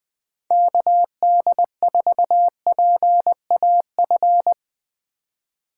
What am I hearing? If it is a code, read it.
Morse code, heard as a keyed tone, about 22 wpm, 700 Hz. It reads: KD4PAF
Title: Speed Racing - USA 2x3 Call Signs 20wpm